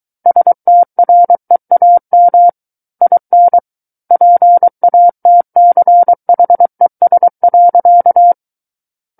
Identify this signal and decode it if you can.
Morse code, heard as a keyed tone, about 23 wpm, 700 Hz. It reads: STREAM IN PATCHES.
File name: Book One / 14-Chapter 14 - 23 WPM